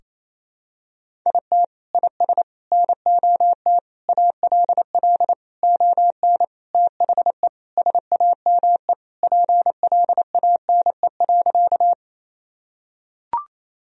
Morse code, heard as a keyed tone, about 27 wpm, 700 Hz. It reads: IT IS NOT ALL ON THE SAME PLANE.
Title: Sentences from Top 1000 Words - Review Part 3 - 28wpm